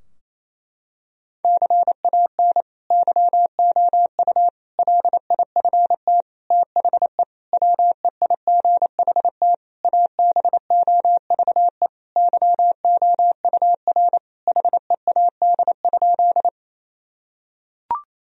Morse code, heard as a keyed tone, about 27 wpm, 700 Hz. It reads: CAN YOU LIFT THE WEIGHT ABOVE YOUR HEAD?
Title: Sentences from Top 1000 Words - Review Part 3 - 28wpm